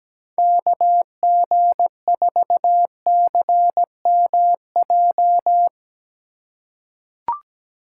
Morse code, heard as a keyed tone, about 17 wpm, 700 Hz. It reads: KG4CMJ
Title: USA Call Signs - 6 Characters - 2x Character Spacing - 17wpm